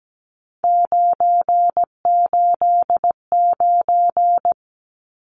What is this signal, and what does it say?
Morse code, heard as a keyed tone, about 17 wpm, 700 Hz. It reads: 989
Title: Numbers - 3-Digit 17wpm